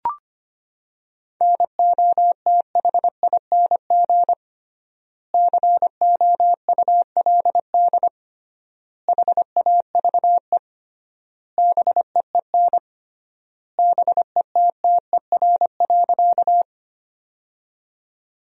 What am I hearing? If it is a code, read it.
Morse code, heard as a keyed tone, about 25 wpm, 700 Hz. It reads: NOTHING COULD HAVE BEEN BETTER.
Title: 3x Word-Spacing - Sentences from Top 500 Words - 25wpm